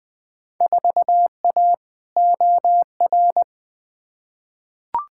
Morse code, heard as a keyed tone, about 20 wpm, 700 Hz. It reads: 4A OR
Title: Field Day Exchanges Encore - 20wpm